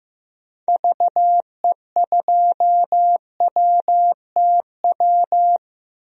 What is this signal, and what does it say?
Morse code, heard as a keyed tone, about 15 wpm, 700 Hz. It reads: VE2WTW
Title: DXCC-100 Call Signs - 6 Characters - Repeated 2x - 15wpm